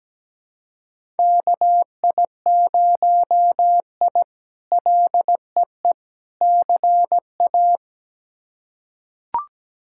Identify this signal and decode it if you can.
Morse code, heard as a keyed tone, about 17 wpm, 700 Hz. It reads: KI0I LEE CA
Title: K1USN SST QSO Practice - 17wpm